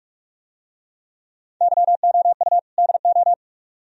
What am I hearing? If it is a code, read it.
Morse code, heard as a keyed tone, about 45 wpm, 700 Hz. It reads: YOU DO